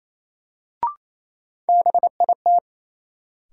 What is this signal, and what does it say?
Morse code, heard as a keyed tone, about 28 wpm, 700 Hz. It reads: BIT
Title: Sets of 3 Words - New Word Drill - 28wpm